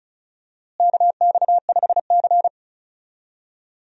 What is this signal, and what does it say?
Morse code, heard as a keyed tone, about 36 wpm, 700 Hz. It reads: KX5C
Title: USA Call Signs - 4 Characters - Repeated 2x - 35wpm